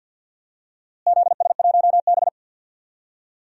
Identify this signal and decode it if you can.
Morse code, heard as a keyed tone, about 46 wpm, 700 Hz. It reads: ZS1B
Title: DXCC-100 Call Signs - 4 Characters - Repeated 3x - 50wpm